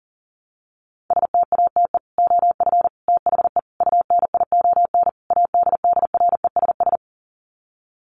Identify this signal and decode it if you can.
Morse code, heard as a keyed tone, about 40 wpm, 700 Hz. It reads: STATE OF THE UNION ADDRESS
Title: Sets of 5 Words Encore 40wpm